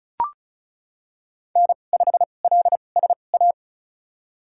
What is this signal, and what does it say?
Morse code, heard as a keyed tone, about 34 wpm, spaced as for 25 wpm, 700 Hz. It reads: N5LSA
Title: USA Call Signs - 2x Character Spacing - 35wpm